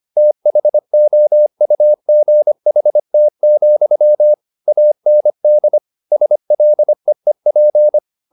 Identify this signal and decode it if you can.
Morse code, heard as a keyed tone, about 25 wpm, 600 Hz. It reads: THOUGHT, AND SLEEP
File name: the old man and the sea part 1 25wpm